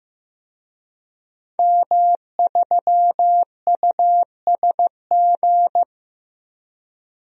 Morse code, heard as a keyed tone, about 15 wpm, 700 Hz. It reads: M3USG
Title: DXCC-100 Call Signs - 5 Characters - 15wpm